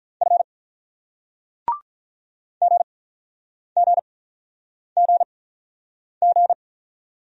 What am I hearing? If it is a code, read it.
Morse code, heard as a keyed tone, about 48 wpm, spaced as for 9 wpm, 700 Hz. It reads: F GGGG